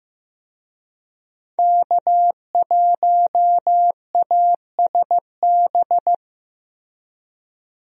Morse code, heard as a keyed tone, about 15 wpm, 700 Hz. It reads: K1ASB
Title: USA Call Signs - 5 Characters - Repeated 3x - 15wpm